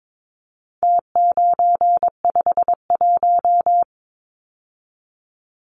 Morse code, heard as a keyed tone, about 22 wpm, 700 Hz. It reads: T951
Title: Numbers - Contest Style - 4-Digit 22wpm